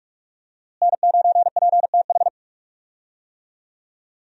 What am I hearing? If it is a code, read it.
Morse code, heard as a keyed tone, about 42 wpm, 700 Hz. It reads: N9PTH
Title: USA Call Signs - 5 Characters - Repeated 2x - 45wpm